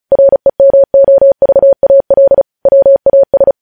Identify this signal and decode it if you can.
Morse code, heard as a keyed tone, about 36 wpm, 550 Hz. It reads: REMOVAL WAS
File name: podcast / News headlines-1x-35x35